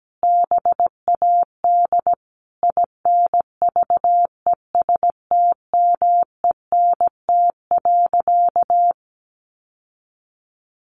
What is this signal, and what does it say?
Morse code, heard as a keyed tone, about 17 wpm, 700 Hz. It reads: BAD INVESTMENT.
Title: Sentences from Top 700 Words 17wpm